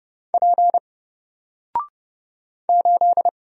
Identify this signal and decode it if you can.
Morse code, heard as a keyed tone, about 30 wpm, 700 Hz. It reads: P 8
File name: Single-Letter-Number-30wpm